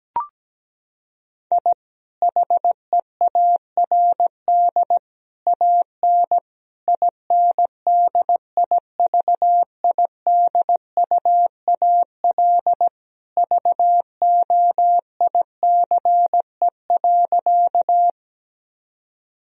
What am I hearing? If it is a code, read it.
Morse code, heard as a keyed tone, about 17 wpm, 700 Hz. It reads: I HEARD AN INDIVIDUAL VOICE.
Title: Sentences from Top 1000 Words - Review Part 4 - 17wpm